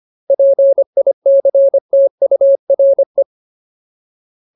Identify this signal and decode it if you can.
Morse code, heard as a keyed tone, about 25 wpm, 550 Hz. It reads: PICTURE
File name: Ultimate Word List / 25WPM Ultimate